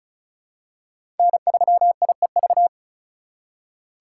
Morse code, heard as a keyed tone, about 36 wpm, 700 Hz. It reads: N3IEV
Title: USA Call Signs - 5 Characters - 6x Character Spacing - 35wpm